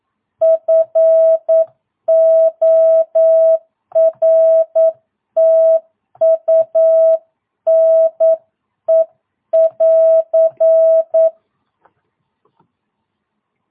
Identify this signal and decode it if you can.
Morse code, heard as a keyed tone, about 9 wpm, 650 Hz. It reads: FORTUNE <AR>
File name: jh8cbh / 2022cw